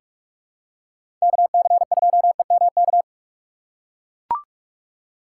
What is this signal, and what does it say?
Morse code, heard as a keyed tone, about 44 wpm, 700 Hz. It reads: KC2EMK
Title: USA Call Signs - 6 Characters - 3x Character Spacing - 45wpm